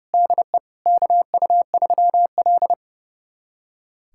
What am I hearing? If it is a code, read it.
Morse code, heard as a keyed tone, about 32 wpm, 700 Hz. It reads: DE KU3L
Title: USA Callsign DE Callsign - 4 Characters - 30wpm